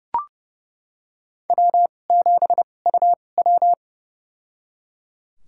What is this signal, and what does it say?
Morse code, heard as a keyed tone, about 30 wpm, spaced as for 22 wpm, 700 Hz. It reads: W7UW
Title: USA Call Signs - 4 Characters - 2x Character Spacing - 30wpm